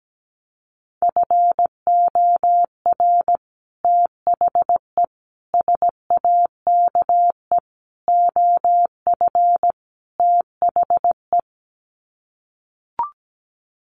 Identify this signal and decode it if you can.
Morse code, heard as a keyed tone, about 17 wpm, 700 Hz. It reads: FOR THE SAKE OF THE
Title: Sets of 5 Words Encore 17wpm